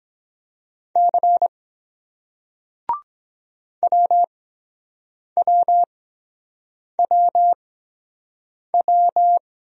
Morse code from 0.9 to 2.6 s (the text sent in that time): C